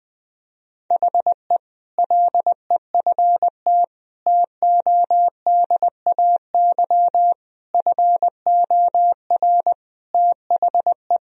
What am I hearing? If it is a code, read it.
Morse code, heard as a keyed tone, about 20 wpm, 700 Hz. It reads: HE LEFT TODAY FOR THE
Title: Sentences from Top 1000 Words - Review Part 2 - 20wpm